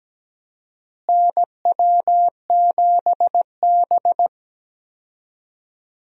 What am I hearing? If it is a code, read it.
Morse code, heard as a keyed tone, about 17 wpm, 700 Hz. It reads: NW7B
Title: USA Call Signs - 4 Characters - Repeated 2x - 17wpm